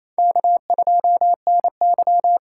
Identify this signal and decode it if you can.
Morse code, heard as a keyed tone, about 34 wpm, 700 Hz. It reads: K2NY
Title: Speed Racing - USA 1x2 Call Signs 28wpm